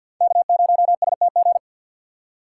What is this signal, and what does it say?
Morse code, heard as a keyed tone, about 47 wpm, 700 Hz. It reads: K0STG
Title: USA Call Signs - 5 Characters - 6x Character Spacing - 50wpm